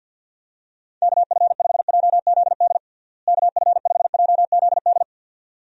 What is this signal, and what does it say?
Morse code, heard as a keyed tone, about 51 wpm, 700 Hz. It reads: KF5JZD KF5JZD
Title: USA Call Signs - 6 Characters - Repeated 2x - 50wpm